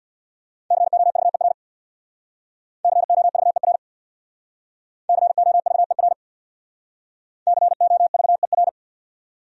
Morse code, heard as a keyed tone, about 62 wpm, 700 Hz. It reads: COVER COVER COVER COVER
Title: Speed-Racing Top 500 Words 50wpm